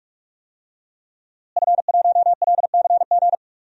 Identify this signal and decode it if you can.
Morse code, heard as a keyed tone, about 41 wpm, 700 Hz. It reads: F1LCG
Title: DXCC-100 Call Signs - 5 Characters - Repeated 3x - 45wpm